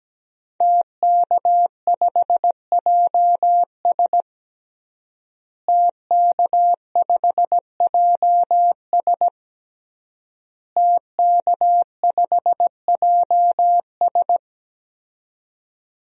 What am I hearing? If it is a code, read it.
Morse code, heard as a keyed tone, about 17 wpm, 700 Hz. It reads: TK5JS TK5JS TK5JS
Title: DXCC-100 Call Signs - 5 Characters - Repeated 3x - 17wpm